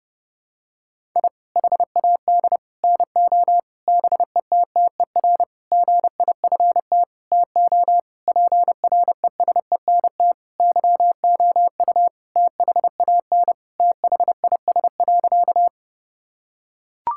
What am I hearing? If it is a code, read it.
Morse code, heard as a keyed tone, about 30 wpm, 700 Hz. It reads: I HAD NO BETTER GIFT TO PRESENT YOU THAN THIS.
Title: Sentences from Top 1000 Words - Review Part 6 - 30wpm